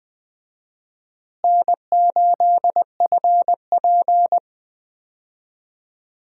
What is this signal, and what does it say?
Morse code, heard as a keyed tone, about 21 wpm, 700 Hz. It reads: N8FP
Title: USA Call Signs - 4 Characters - Repeated 3x - 20wpm